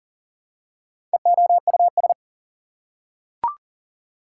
Morse code, heard as a keyed tone, about 40 wpm, 700 Hz. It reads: EOUS
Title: Prefixes and Suffixes 40wpm